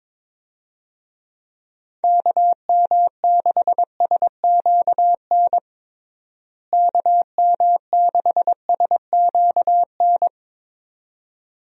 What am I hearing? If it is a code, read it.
Morse code, heard as a keyed tone, about 22 wpm, 700 Hz. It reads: KM6SQN KM6SQN
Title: USA Call Signs - 6 Characters - Repeated 2x - 22wpm